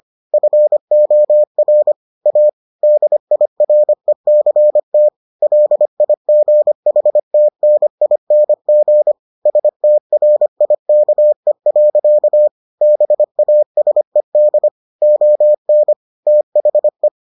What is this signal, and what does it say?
Morse code, heard as a keyed tone, about 25 wpm, 600 Hz. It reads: FOR A DIRECT LIGHTNING STRIKE. BASED ON THE